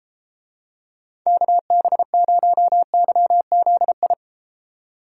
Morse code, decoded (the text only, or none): KB0YZI